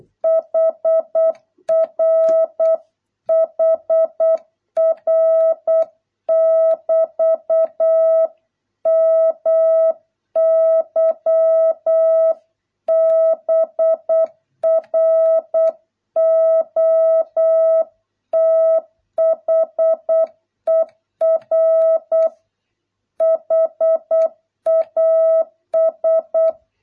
Morse code, heard as a keyed tone, about 8 wpm, 650 Hz. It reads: HR HR <BT> MY BROTHER HAS